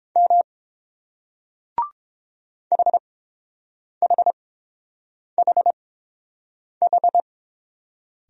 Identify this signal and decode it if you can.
Morse code, heard as a keyed tone, about 30 wpm, spaced as for 7 wpm, 700 Hz. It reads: M HHHH